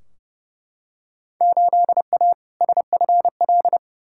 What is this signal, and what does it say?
Morse code, heard as a keyed tone, about 30 wpm, 700 Hz. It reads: 8A SFL
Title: Field Day Exchanges Encore - 30wpm